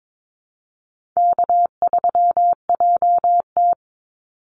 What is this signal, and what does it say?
Morse code, heard as a keyed tone, about 22 wpm, 700 Hz. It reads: K3JT
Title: Easy Call Signs Repeated 2x - 22wpm